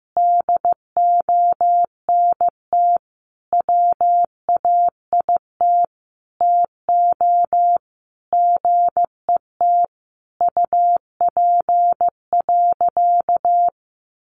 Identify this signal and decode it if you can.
Morse code, heard as a keyed tone, about 15 wpm, 700 Hz. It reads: DONT WAIT TO GET UP.
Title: Sentences from Top 500 Words 15wpm